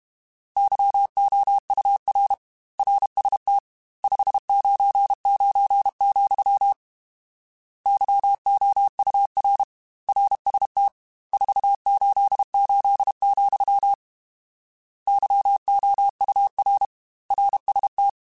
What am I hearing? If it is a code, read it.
Morse code, heard as a keyed tone, about 32 wpm, 800 Hz. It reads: YOUR RST 599, YOUR RST 488, YOUR RST
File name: RSTs / RST-mp3s / Your RSTs @30